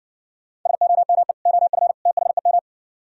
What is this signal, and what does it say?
Morse code, heard as a keyed tone, about 55 wpm, 700 Hz. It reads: SOME OF THEM